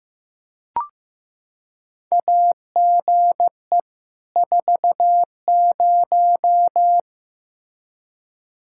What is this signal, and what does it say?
Morse code, heard as a keyed tone, about 15 wpm, 700 Hz. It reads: AGE 40
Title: Practice QSO - Exchange 3 - More Info - 15wpm